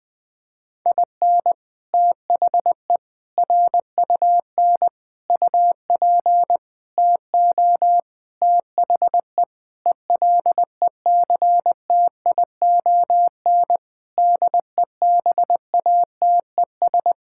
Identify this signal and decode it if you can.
Morse code, heard as a keyed tone, about 20 wpm, 700 Hz. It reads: IN THE RUN UP TO THE ELECTION DEBATES
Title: Sentences with 6-Word Phrases - Spoken Beginning - 20wpm